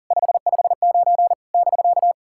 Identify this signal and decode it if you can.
Morse code, heard as a keyed tone, about 44 wpm, 700 Hz. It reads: 559 <BK>
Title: SOTA QSO Practice 40wpm